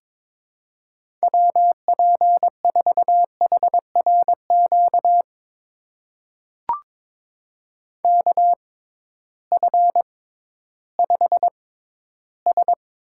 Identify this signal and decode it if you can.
Morse code, heard as a keyed tone, about 22 wpm, spaced as for 13 wpm, 700 Hz. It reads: WP4HRQ KF5S
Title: USA Call Signs - 6 Characters - 6x Character Spacing - 22wpm